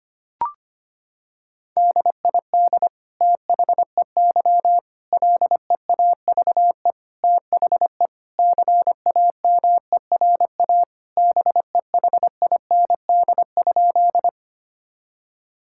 Morse code, heard as a keyed tone, about 25 wpm, 700 Hz. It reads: DID THEY LEAVE THE CAMERA BEHIND?
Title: Sentences from Top 1000 Words - Review Part 5 - 25wpm